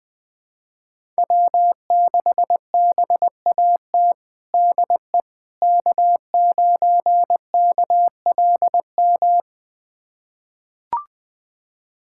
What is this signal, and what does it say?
Morse code, heard as a keyed tone, about 21 wpm, 700 Hz. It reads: W6BAT DE K9KLM
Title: USA Callsign DE Callsign - 5 Characters - 20wpm